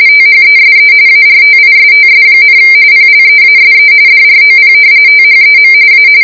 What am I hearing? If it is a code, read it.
RTTY, about 45.45 baud. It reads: THESE ARTISTS, PLEASE DROP THEM A